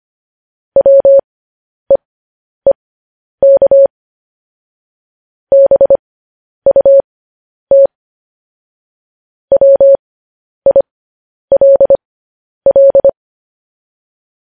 Morse code, heard as a keyed tone, about 25 wpm, spaced as for 10 wpm, 550 Hz. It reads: WEEK BUT WILL
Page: News Headlines in Morse Code at 10 WPM